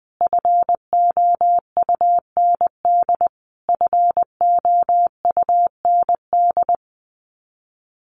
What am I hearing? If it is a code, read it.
Morse code, heard as a keyed tone, about 19 wpm, 700 Hz. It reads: FOUND FOUND
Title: Top 300 Words Repeated 20wpm